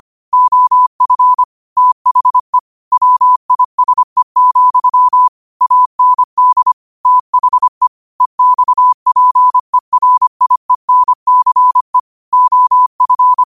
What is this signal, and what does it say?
Morse code, heard as a keyed tone, about 25 wpm, 1000 Hz. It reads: OF THE WISE, AND THE EXPERIENCE OF
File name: qotdfiles / 25wpm